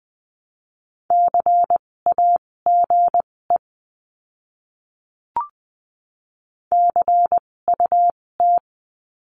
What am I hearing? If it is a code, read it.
Morse code, heard as a keyed tone, about 20 wpm, spaced as for 16 wpm, 700 Hz. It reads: CAGE CUT
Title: 256 Lesson 42a - Words 12-20wpm